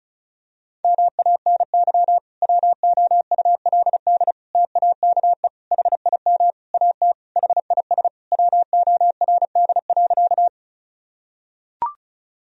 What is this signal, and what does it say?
Morse code, heard as a keyed tone, about 36 wpm, 700 Hz. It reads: MANY WOULD TAKE HIM AT HIS WORD.
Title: Sentences from Top 1000 Words - Review Part 1 - 35wpm